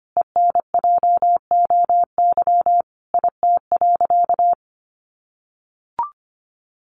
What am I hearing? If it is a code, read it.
Morse code, heard as a keyed tone, about 25 wpm, 700 Hz. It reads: ENJOY IT.
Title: Sentences from Top 1500 Words 25wpm